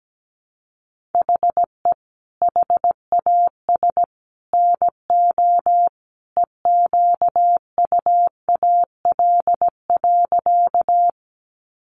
Morse code, heard as a keyed tone, about 17 wpm, 700 Hz. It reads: HE HAS NO EQUAL.
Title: Sentences from Top 1400 Words 17wpm